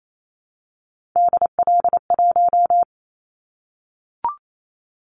Morse code, heard as a keyed tone, about 28 wpm, 700 Hz. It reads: DL1
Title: Callsign Prefixes for Top 25 DXCC - 28wpm